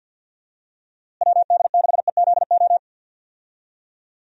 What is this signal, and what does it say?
Morse code, heard as a keyed tone, about 50 wpm, 700 Hz. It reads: WD6EZO
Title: USA Call Signs - 6 Characters - Repeated 3x - 50wpm